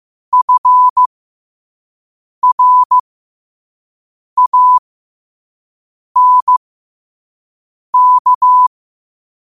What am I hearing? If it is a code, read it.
Morse code, heard as a keyed tone, about 15 wpm, spaced as for 5 wpm, 1000 Hz. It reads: FRANK
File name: qotdfiles / 5wpm